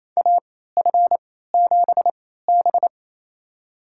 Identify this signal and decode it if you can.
Morse code, heard as a keyed tone, about 28 wpm, spaced as for 16 wpm, 700 Hz. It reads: AF7B
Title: USA Call Signs - 4 Characters - 3x Character Spacing - 28wpm